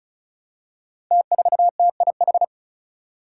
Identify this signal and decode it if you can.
Morse code, heard as a keyed tone, about 36 wpm, 700 Hz. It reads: T4TIH
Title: Five Challenging Letter-Number Combinations 35wpm